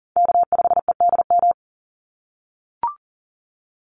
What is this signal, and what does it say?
Morse code, heard as a keyed tone, about 40 wpm, 700 Hz. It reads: K5EDM
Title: Call Signs - 40wpm